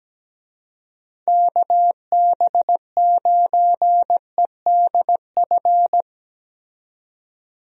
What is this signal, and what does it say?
Morse code, heard as a keyed tone, about 17 wpm, 700 Hz. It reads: KB9EDF
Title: USA Call Signs - 6 Characters - 17wpm